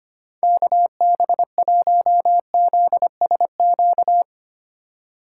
Morse code, heard as a keyed tone, about 32 wpm, 700 Hz. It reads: KB1ZSQ